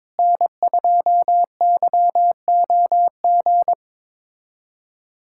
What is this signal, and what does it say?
Morse code, heard as a keyed tone, about 21 wpm, 700 Hz. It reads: N2YOG
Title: USA Callsign DE Callsign 22wpm